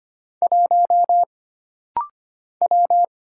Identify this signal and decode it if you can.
Morse code, heard as a keyed tone, about 25 wpm, 700 Hz. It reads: 1 W
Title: Single Letter-Number - Rapid-Fire 25wpm